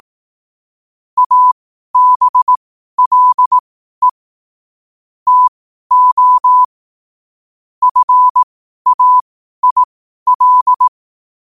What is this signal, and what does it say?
Morse code, heard as a keyed tone, about 18 wpm, spaced as for 12 wpm, 1000 Hz. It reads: ABLE TO FAIL
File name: qotdfiles / 12wpm